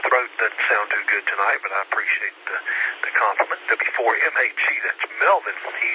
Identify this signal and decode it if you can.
Emergency Alert System (SAME) tones (header bursts), heard as harsh buzzing bursts, with speech. It is heard from 0.3 to 5.2 s.